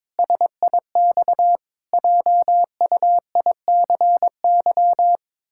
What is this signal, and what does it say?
Morse code, heard as a keyed tone, about 22 wpm, 700 Hz. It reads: SIX JUICY
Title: Pangram Challenge v2 - 22wpm